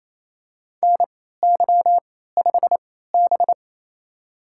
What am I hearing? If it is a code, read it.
Morse code, heard as a keyed tone, about 28 wpm, spaced as for 16 wpm, 700 Hz. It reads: NY5B
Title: USA Call Signs - 4 Characters - 3x Character Spacing - 28wpm